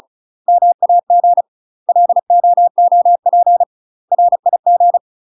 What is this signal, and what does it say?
Morse code, heard as a keyed tone, about 34 wpm, 700 Hz. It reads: MAG LOOP RIG
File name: UK MASTER QSOs TRAINING / 35WPM / UK MASTER QSOs 0001